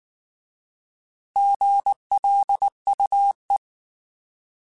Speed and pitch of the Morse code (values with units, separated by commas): 19 wpm, 800 Hz